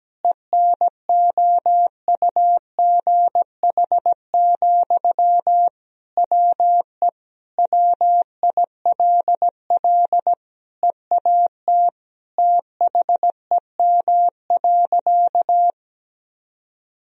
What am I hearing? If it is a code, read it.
Morse code, heard as a keyed tone, about 17 wpm, 700 Hz. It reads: ENOUGH, WE WILL EAT THEM.
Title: Sentences from Top 1000 Words - Review Part 4 - 17wpm